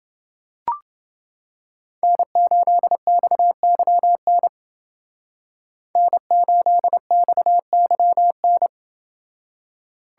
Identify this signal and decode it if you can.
Morse code, heard as a keyed tone, about 30 wpm, 700 Hz. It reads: N8XYN N8XYN